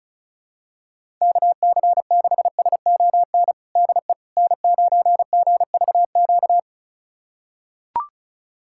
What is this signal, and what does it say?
Morse code, heard as a keyed tone, about 36 wpm, 700 Hz. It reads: KC6SON DE N9GVQ
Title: USA Callsign DE Callsign 35wpm